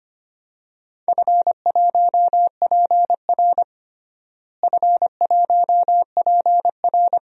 Morse code, heard as a keyed tone, about 25 wpm, 700 Hz. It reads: F1PR F1PR
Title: DXCC-100 Call Signs - 4 Characters - Repeated 3x - 25wpm